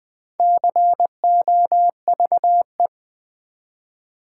Morse code, heard as a keyed tone, about 20 wpm, 700 Hz. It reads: COVE
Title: Build A Word - 4 Letters At A Time - 8-Letter Words - 20wpm